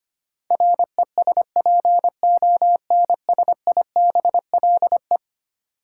Morse code, heard as a keyed tone, about 25 wpm, 700 Hz. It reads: RESPONSIBLE
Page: morse-code-ninja-practice-25wpm